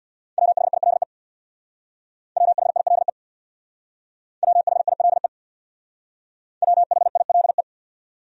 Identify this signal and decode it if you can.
Morse code, heard as a keyed tone, about 63 wpm, 700 Hz. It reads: WHILE WHILE WHILE WHILE